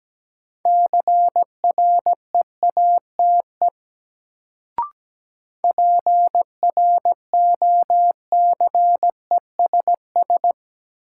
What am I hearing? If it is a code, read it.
Morse code, heard as a keyed tone, about 17 wpm, 700 Hz. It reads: CREATE PROCESS
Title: Sentences from Top 1000 Words - Review Part 3 - New Word Drill - 17wpm